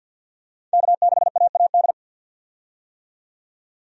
Morse code, heard as a keyed tone, about 53 wpm, 700 Hz. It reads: K6AAD
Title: USA Call Signs - 5 Characters - 50wpm